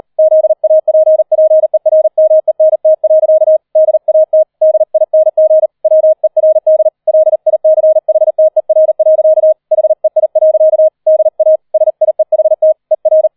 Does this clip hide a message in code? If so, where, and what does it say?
Yes: Morse code from 0.2 to 13.3 s, heard as a keyed tone, about 38 wpm, 630 Hz. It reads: ZAPPERMENT. DAT DING WERD LICHTER. HEI. DA SIEHT ER